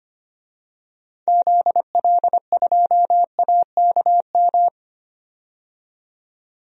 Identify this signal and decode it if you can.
Morse code, heard as a keyed tone, about 25 wpm, 700 Hz. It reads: ZL2AKM
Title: DXCC-100 Call Signs - 6 Characters - 25wpm